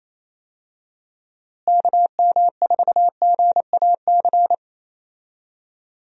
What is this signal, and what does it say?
Morse code, heard as a keyed tone, about 27 wpm, 700 Hz. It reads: KM4GAC
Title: USA Call Signs - 6 Characters - Repeated 3x - 28wpm